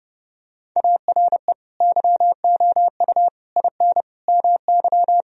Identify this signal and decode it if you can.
Morse code, heard as a keyed tone, about 32 wpm, 700 Hz. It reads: ARE YOU IN MY
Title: Sentences from Top 1000 Words - Review Part 2 - 30wpm